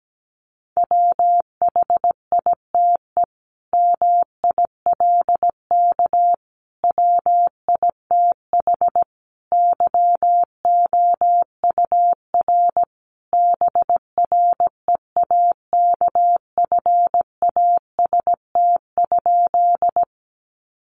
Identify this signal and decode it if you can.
Morse code, heard as a keyed tone, about 17 wpm, 700 Hz. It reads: WHITE MILK WITH YOUR BREAKFAST?
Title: Sentences from Top 1500 Words 17wpm